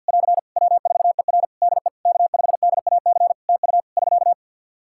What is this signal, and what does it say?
Morse code, heard as a keyed tone, about 50 wpm, 700 Hz. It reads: <AR> W4ER DE K5DAC TU <SK>
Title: K1USN SST QSO Practice 50wpm